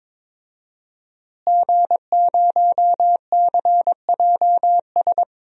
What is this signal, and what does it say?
Morse code, heard as a keyed tone, about 22 wpm, 700 Hz. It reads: G0CJS
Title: DXCC-100 Call Signs - 5 Characters - Repeated 2x - 22wpm